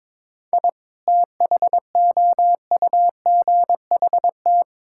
Morse code, heard as a keyed tone, about 22 wpm, 700 Hz. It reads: I THOUGHT
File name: practice / 22wpm / Sentences with 2-Word Phrases - 22wpm